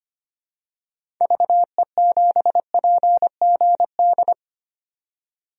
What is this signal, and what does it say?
Morse code, heard as a keyed tone, about 25 wpm, 700 Hz. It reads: VE7PGD
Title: DXCC-100 Call Signs - 6 Characters - 25wpm